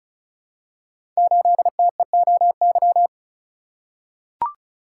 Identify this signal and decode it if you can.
Morse code, heard as a keyed tone, about 35 wpm, 690 Hz. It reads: 8TEOY
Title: Code Groups - 5 Letter:Numbers - 35wpm